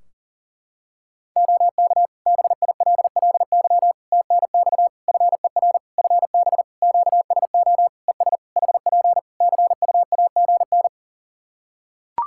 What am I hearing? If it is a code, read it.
Morse code, heard as a keyed tone, about 40 wpm, 700 Hz. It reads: OK BILLY TNX FER FB QSO ES HP CUAGN
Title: Practice QSO - Exchange 4 - Ending - 40wpm